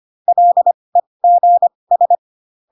Morse code, heard as a keyed tone, about 25 wpm, spaced as for 20 wpm, 700 Hz. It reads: LEGS